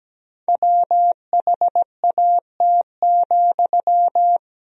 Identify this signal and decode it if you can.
Morse code, heard as a keyed tone, about 17 wpm, 700 Hz. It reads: WHAT,
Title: Sentences from Top 1000 Words - Review Part 4 - 17wpm